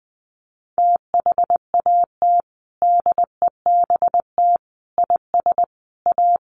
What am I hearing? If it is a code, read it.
Morse code, heard as a keyed tone, about 20 wpm, 700 Hz. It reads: THAT DEBT IS A
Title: Sentences from Top 1300 Words 20wpm